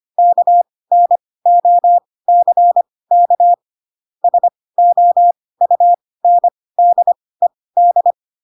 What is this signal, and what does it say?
Morse code, heard as a keyed tone, about 25 wpm, spaced as for 18 wpm, 700 Hz. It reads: KNOCK SOUNDED
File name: Pinocchio / Pinocchio ch2 12 25 ik1hge